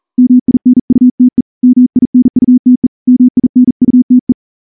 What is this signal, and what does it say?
Morse code, heard as a keyed tone, about 38 wpm, 250 Hz. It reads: MINUTE MINUTE MINUTE
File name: morse-training-40WPM